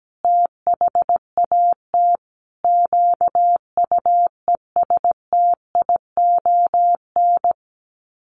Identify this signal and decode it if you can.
Morse code, heard as a keyed tone, about 17 wpm, 700 Hz. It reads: THAT QUESTION
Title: Sets of 5 Words Encore2 17wpm